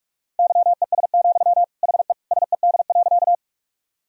Morse code, heard as a keyed tone, about 42 wpm, 700 Hz. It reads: YES, HE SED.
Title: Sentences from Top 500 Words and Abbreviations 45wpm